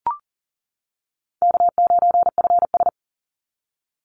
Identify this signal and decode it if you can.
Morse code, heard as a keyed tone, about 40 wpm, 700 Hz. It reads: K9FS
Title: Easy Call Signs - 40wpm